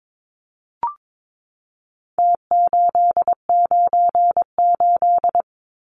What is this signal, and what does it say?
Morse code, heard as a keyed tone, about 22 wpm, 700 Hz. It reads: T898
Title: Numbers - Contest Style - 4-Digit 22wpm